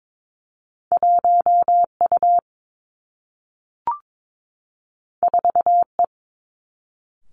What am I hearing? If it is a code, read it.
Morse code, heard as a keyed tone, about 22 wpm, 700 Hz. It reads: 1 U 4 E
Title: Two Challenging Letter-Number Combinations 22wpm